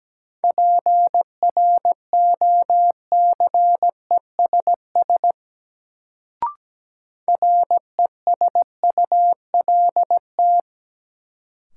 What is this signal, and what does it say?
Morse code, heard as a keyed tone, about 17 wpm, 700 Hz. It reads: PROCESS RESULT
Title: Sets of 3 Words - New Word Drill - 17wpm